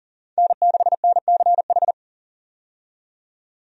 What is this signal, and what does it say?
Morse code, heard as a keyed tone, about 40 wpm, 700 Hz. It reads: N6NCH
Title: USA Call Signs - 5 Characters - 40wpm